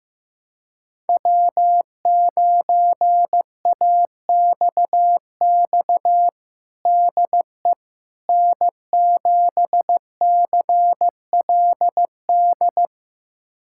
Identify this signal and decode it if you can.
Morse code, heard as a keyed tone, about 15 wpm, 700 Hz. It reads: W9AXX DE N7CLD
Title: USA Callsign DE Callsign - 5 Characters - 15wpm